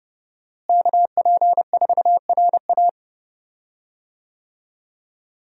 Morse code, heard as a keyed tone, about 30 wpm, 700 Hz. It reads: KP4RA
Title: DXCC-100 Call Signs - 5 Characters - Repeated 3x - 30wpm